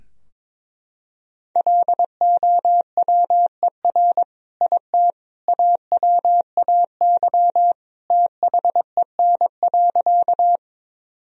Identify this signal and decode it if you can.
Morse code, heard as a keyed tone, about 22 wpm, 700 Hz. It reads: LOWER IT AWAY THEN.
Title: Sentences from Top 1000 Words - Review Part 4 - 22wpm